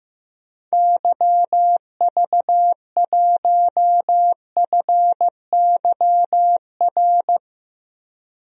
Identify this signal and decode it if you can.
Morse code, heard as a keyed tone, about 15 wpm, 700 Hz. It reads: YV1FYR
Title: DXCC-100 Call Signs - 6 Characters - 15wpm